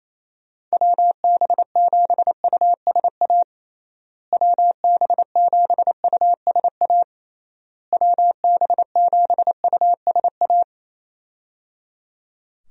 Morse code, heard as a keyed tone, about 28 wpm, 700 Hz. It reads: WB7USA WB7USA WB7USA
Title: USA Call Signs - 6 Characters - Repeated 3x - 28wpm